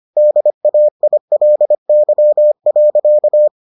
Morse code, heard as a keyed tone, about 25 wpm, 600 Hz. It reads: DAILY.